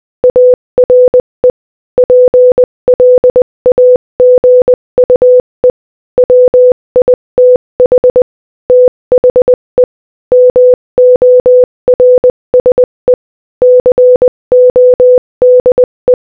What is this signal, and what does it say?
Morse code, heard as a keyed tone, about 20 wpm, 500 Hz. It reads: ARE PLAGUE WITH THE MORSE CODE
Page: Morse Code Audio Practice at Different Speeds